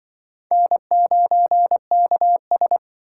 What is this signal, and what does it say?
Morse code, heard as a keyed tone, about 24 wpm, 700 Hz. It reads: N9KS
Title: Speed Racing - USA 1x2 Call Signs 20wpm